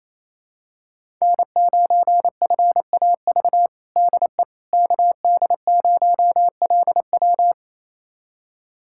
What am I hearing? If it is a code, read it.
Morse code, heard as a keyed tone, about 28 wpm, 700 Hz. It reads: N9FAV DE KD0LW
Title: USA Callsign DE Callsign - 5 Characters - 28wpm